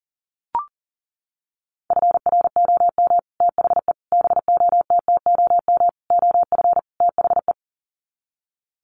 Morse code, heard as a keyed tone, about 40 wpm, 700 Hz. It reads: FROM THE BOTTOM OF THE
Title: Sets of 5 Words Encore2 40wpm